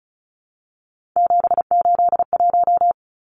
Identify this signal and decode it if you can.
Morse code, heard as a keyed tone, about 34 wpm, 700 Hz. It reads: 781
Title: Numbers - 3-Digit 35wpm